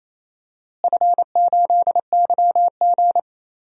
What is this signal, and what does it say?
Morse code, heard as a keyed tone, about 28 wpm, 700 Hz. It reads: F8YG
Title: DXCC-100 Call Signs - 4 Characters - Repeated 2x - 28wpm